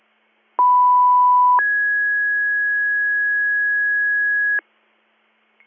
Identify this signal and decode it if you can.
Two-tone page, heard as a one-second tone, then a three-second tone, about 990 then 1650 Hz, with noise. It is heard from 0.6 to 4.6 s.